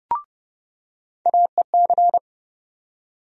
Morse code, heard as a keyed tone, about 31 wpm, 700 Hz. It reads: AEC
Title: USA Call Sign Postfixes - 3 Letters 30wpm